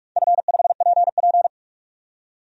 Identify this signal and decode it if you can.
Morse code, heard as a keyed tone, about 46 wpm, 700 Hz. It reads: F5PP
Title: DXCC-100 Call Signs - 4 Characters - 45wpm